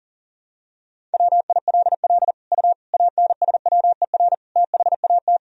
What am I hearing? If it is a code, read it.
Morse code, heard as a keyed tone, about 40 wpm, 700 Hz. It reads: WILL U ANSWER THAT
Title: Sentences from Top 400 Words and Abbreviations 40wpm